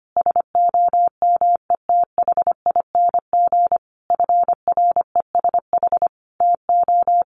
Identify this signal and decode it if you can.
Morse code, heard as a keyed tone, about 25 wpm, 700 Hz. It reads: SOMETHING FRESH TO
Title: Sentences from Top 1200 Words 25wpm